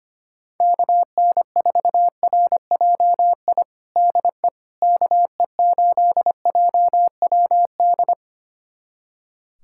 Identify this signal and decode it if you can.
Morse code, heard as a keyed tone, about 25 wpm, 700 Hz. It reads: KN4RJI DE KE8JWD
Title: USA Callsign DE Callsign - 6 Characters - 25wpm